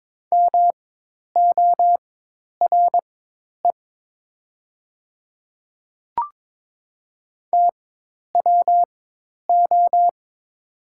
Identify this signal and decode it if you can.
Morse code, heard as a keyed tone, about 22 wpm, spaced as for 10 wpm, 700 Hz. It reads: MORE TWO
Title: 4x Character Spacing - Top 100 Words - 22wpm